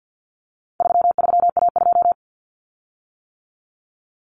Morse code, heard as a keyed tone, about 48 wpm, 700 Hz. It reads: 33A2
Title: Common Cut Numbers - 4-Digit 50wpm